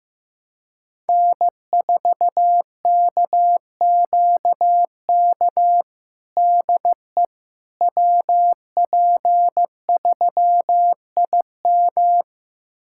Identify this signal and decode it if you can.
Morse code, heard as a keyed tone, about 15 wpm, 700 Hz. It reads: N4KQK DE WP3IM
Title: USA Callsign DE Callsign - 5 Characters - 15wpm